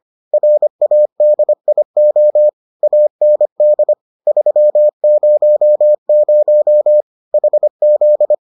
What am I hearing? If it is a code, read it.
Morse code, heard as a keyed tone, about 25 wpm, 600 Hz. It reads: RADIO AND 300 HZ